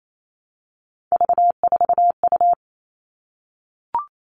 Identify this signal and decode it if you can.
Morse code, heard as a keyed tone, about 28 wpm, 700 Hz. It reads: V4U
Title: All Cut Numbers - 3-Digit 28wpm